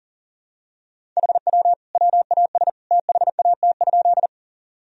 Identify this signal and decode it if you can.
Morse code, heard as a keyed tone, about 40 wpm, 700 Hz. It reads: HW WAS THAT?
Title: Sentences from Top 100 Words and Abbreviations 40wpm